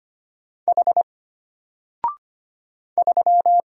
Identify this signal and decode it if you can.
Morse code, heard as a keyed tone, about 25 wpm, 700 Hz. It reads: H 3
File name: Single-Letter-Number-25wpm